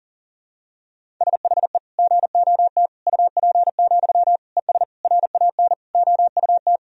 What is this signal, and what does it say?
Morse code, heard as a keyed tone, about 40 wpm, 700 Hz. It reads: SHE GOT UP, ES RAN OUT
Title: Sentences from Top 500 Words and Abbreviations 40wpm